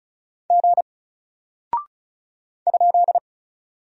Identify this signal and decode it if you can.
Morse code, heard as a keyed tone, about 36 wpm, 700 Hz. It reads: G ?
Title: Single Character (Letter:Number:Punctuation:Prosign) Alt 35wpm